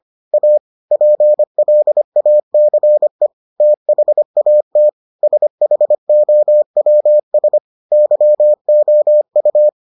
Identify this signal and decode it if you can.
Morse code, heard as a keyed tone, about 25 wpm, 600 Hz. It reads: A PLACE THAT SHOWS YOU